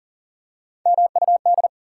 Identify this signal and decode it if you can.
Morse code, heard as a keyed tone, about 40 wpm, 700 Hz. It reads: MUD
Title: Three Letter Words - 40wpm